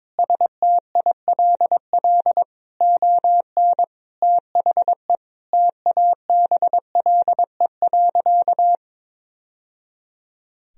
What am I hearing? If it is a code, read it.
Morse code, heard as a keyed tone, about 22 wpm, 700 Hz. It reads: STILL ON THE TABLE.
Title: Sentences from Top 1000 Words - Review Part 4 - 22wpm